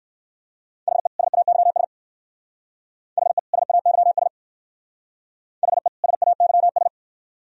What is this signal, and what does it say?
Morse code, heard as a keyed tone, about 59 wpm, 700 Hz. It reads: HE SAYS HE SAYS HE SAYS